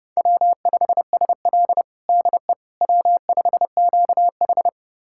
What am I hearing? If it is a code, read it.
Morse code, heard as a keyed tone, about 30 wpm, 700 Hz. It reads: W5SL DE W5QH
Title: USA Callsign DE Callsign - 4 Characters - 30wpm